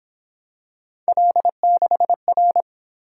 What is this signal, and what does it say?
Morse code, heard as a keyed tone, about 28 wpm, 700 Hz. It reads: L6R
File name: Speed-Racing - Three Letter-Number Combinations - 22wpm